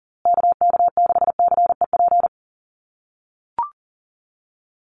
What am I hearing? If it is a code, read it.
Morse code, heard as a keyed tone, about 40 wpm, 700 Hz. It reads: KK6CEP
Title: USA Call Signs Repeated 3x 40wpm